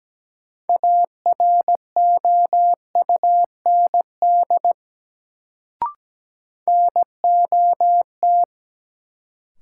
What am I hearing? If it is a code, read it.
Morse code, heard as a keyed tone, about 17 wpm, 700 Hz. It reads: AROUND NOT
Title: Sentences from Top 600 Words - New Word Drill - 17wpm